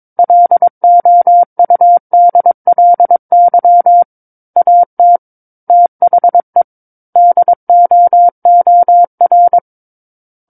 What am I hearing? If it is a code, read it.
Morse code, heard as a keyed tone, about 22 wpm, 700 Hz. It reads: LOUDLY AT THE DOOR
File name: 06-Chapter-06 22 WPM